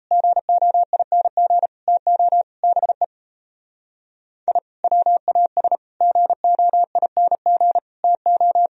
Morse code, heard as a keyed tone, about 36 wpm, 700 Hz. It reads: GOING TO BE I WAS GOING TO